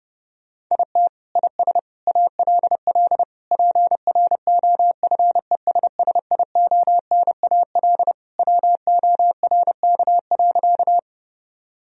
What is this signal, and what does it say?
Morse code, heard as a keyed tone, about 30 wpm, 700 Hz. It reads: IT IS ALL PROFESSIONAL WORK.
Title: Sentences from Top 1000 Words - Review Part 4 - 30wpm